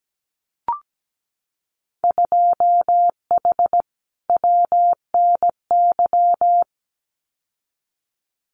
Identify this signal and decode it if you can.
Morse code, heard as a keyed tone, about 17 wpm, 700 Hz. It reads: 2H WNY
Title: Winter Field Day Exchanges Encore 17wpm